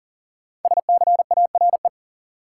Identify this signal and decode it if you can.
Morse code, heard as a keyed tone, about 40 wpm, 700 Hz. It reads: SCARE
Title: Sentences from Top 1000 Words - Review Part 5 - New Word Drill - 40wpm